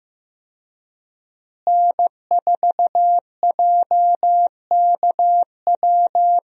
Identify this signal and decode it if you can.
Morse code, heard as a keyed tone, about 15 wpm, 700 Hz. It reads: N4JKW
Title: USA Call Signs - 5 Characters - Repeated 3x - 15wpm